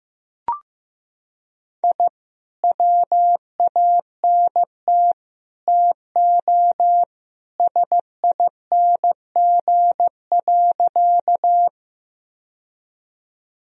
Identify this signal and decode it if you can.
Morse code, heard as a keyed tone, about 15 wpm, 700 Hz. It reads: I WANT TO SING.
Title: Sentences from Top 1000 Words - Review Part 2 - 15wpm